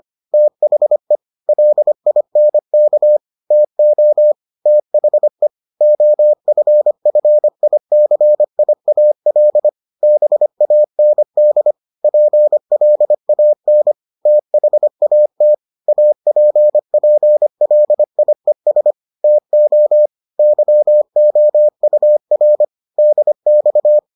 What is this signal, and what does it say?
Morse code, heard as a keyed tone, about 25 wpm, 600 Hz. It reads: THE LINK TO THE OFFICIAL BAND PLAN THAT APPLIES TO YOUR DX